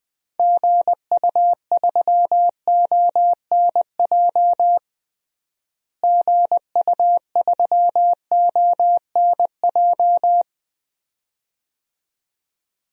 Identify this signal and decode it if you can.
Morse code, heard as a keyed tone, about 21 wpm, 700 Hz. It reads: GU3ONJ GU3ONJ
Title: DXCC-100 Call Signs - 6 Characters - Repeated 2x - 20wpm